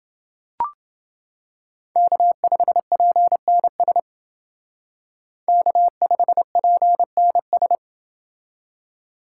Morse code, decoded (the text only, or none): K5PNS K5PNS